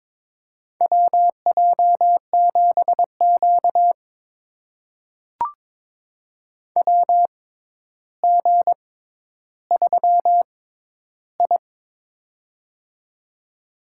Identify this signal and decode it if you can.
Morse code, heard as a keyed tone, about 22 wpm, spaced as for 11 wpm, 700 Hz. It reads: WJ7Q WG3I
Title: USA Call Signs - 4 Characters - 6x Character Spacing - 22wpm